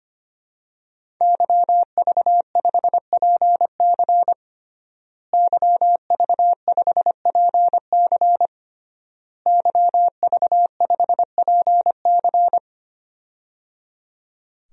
Morse code, heard as a keyed tone, about 25 wpm, 700 Hz. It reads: YV5PC YV5PC YV5PC
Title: DXCC-100 Call Signs - 5 Characters - Repeated 3x - 25wpm